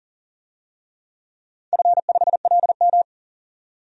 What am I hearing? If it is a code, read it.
Morse code, heard as a keyed tone, about 39 wpm, 700 Hz. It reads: F5LM
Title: DXCC-100 Call Signs - 4 Characters - 40wpm